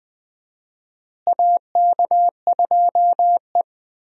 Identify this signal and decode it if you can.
Morse code, heard as a keyed tone, about 20 wpm, 700 Hz. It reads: AK2E
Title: USA Call Signs - 4 Characters - Repeated 2x - 20wpm